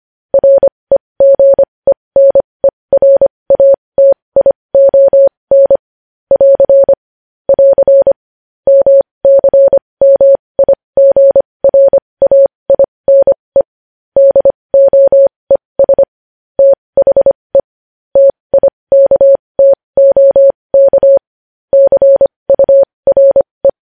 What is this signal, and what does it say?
Morse code, heard as a keyed tone, about 25 wpm, spaced as for 20 wpm, 550 Hz. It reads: REGENERATION <AR> <AR> MCMIGRAINE DOES THE TIKTOK CURE
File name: podcast / News headlines-1x-25x20